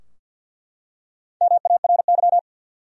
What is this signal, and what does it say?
Morse code, heard as a keyed tone, about 49 wpm, 700 Hz. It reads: MARY
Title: USA Common Names - 50wpm